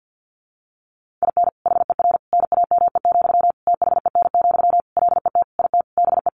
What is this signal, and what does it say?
Morse code, heard as a keyed tone, about 50 wpm, 700 Hz. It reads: IN HER NAME, THEN, LET IT BE
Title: Sentences from Top 500 Words 50wpm